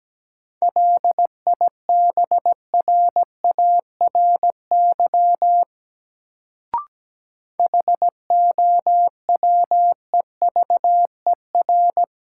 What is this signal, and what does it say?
Morse code, heard as a keyed tone, about 17 wpm, 700 Hz. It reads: LIBRARY HOWEVER
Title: Sentences from Top 600 Words - New Word Drill - 17wpm